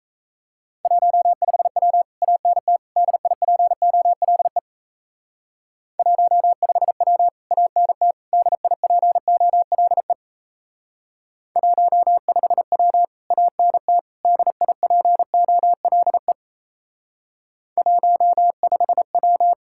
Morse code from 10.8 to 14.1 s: 15W ANT

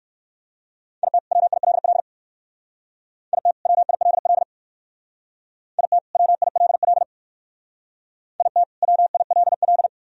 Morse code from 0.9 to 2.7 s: IT WILL

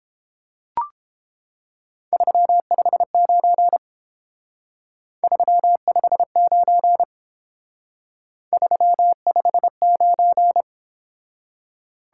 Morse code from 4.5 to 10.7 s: 359 359

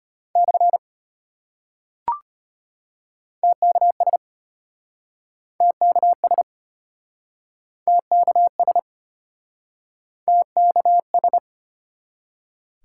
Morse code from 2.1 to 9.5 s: TKS TKS TKS

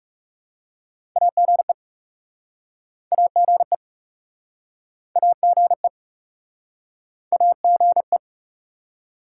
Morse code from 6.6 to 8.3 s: AGE

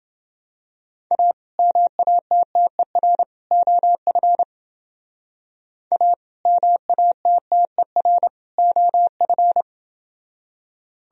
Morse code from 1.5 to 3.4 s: MATTER

